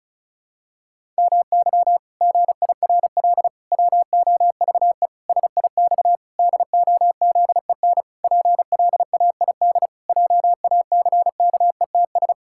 Morse code, heard as a keyed tone, about 34 wpm, 700 Hz. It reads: MY GIRL WOVE SIX DOZEN PLAID JACKETS
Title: Pangram Challenge v2 - 35wpm